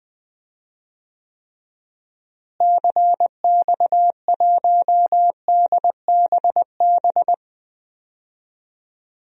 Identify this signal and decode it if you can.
Morse code, heard as a keyed tone, about 20 wpm, 700 Hz. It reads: CX1DBB
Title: DXCC-100 Call Signs - 6 Characters - Repeated 2x - 20wpm